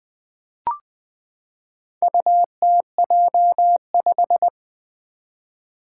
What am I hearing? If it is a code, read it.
Morse code, heard as a keyed tone, about 20 wpm, 700 Hz. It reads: UTJ5
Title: Four Challenging Letter-Number Combinations 20wpm